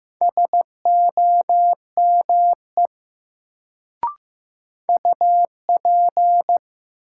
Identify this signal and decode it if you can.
Morse code, heard as a keyed tone, about 15 wpm, 700 Hz. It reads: SOME UP
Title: Sentences from Top 100 Words - New Word Drill - 15wpm